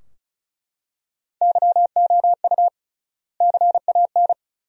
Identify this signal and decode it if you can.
Morse code, heard as a keyed tone, about 34 wpm, 700 Hz. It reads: YOU CAN
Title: 3x Word-Spacing - Sentences from Top 100 Words - 35wpm